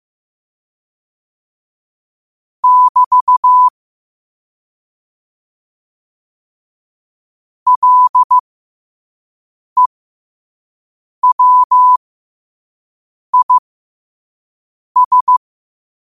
Morse code, heard as a keyed tone, about 15 wpm, spaced as for 5 wpm, 1000 Hz. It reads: <BT> LEWIS
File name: qotdfiles / 5wpm